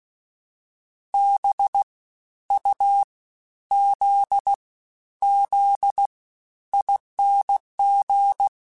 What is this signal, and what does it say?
Morse code, heard as a keyed tone, about 16 wpm, 800 Hz. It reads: B U Z Z ING
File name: Special INGs-mp3s / Special INGs-mp3s / SlowToFaster ing @15